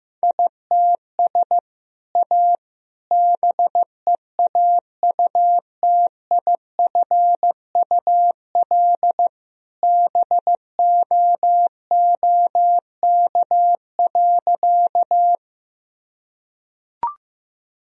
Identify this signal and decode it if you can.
Morse code, heard as a keyed tone, about 15 wpm, 700 Hz. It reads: ITS A BEAUTIFUL BOOK.
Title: Sentences from Top 1000 Words - Review Part 1 - 15wpm